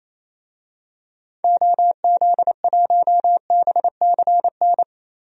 Morse code, heard as a keyed tone, about 27 wpm, 700 Hz. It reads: OZ1BCN
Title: DXCC-100 Call Signs - 6 Characters - Repeated 3x - 28wpm